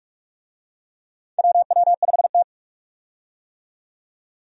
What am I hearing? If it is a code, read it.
Morse code, heard as a keyed tone, about 45 wpm, 700 Hz. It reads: WW5T
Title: USA Call Signs - 4 Characters - 45wpm